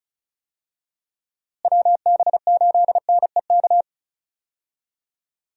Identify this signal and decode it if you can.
Morse code, heard as a keyed tone, about 36 wpm, 700 Hz. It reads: WB8NEK
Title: USA Call Signs - 6 Characters - Repeated 2x - 35wpm